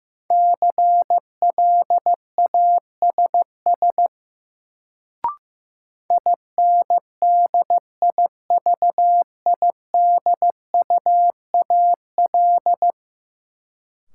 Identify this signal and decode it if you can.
Morse code, heard as a keyed tone, about 16 wpm, 700 Hz. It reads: CLASS INDIVIDUAL
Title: Sentences from Top 1400 Words - New Word Drill Alt - 15wpm